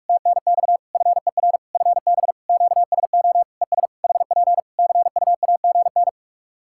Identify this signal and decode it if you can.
Morse code, heard as a keyed tone, about 46 wpm, 700 Hz. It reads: TNX FER FB QSO ES HP CUAGN
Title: K1USN SST QSO Practice 45wpm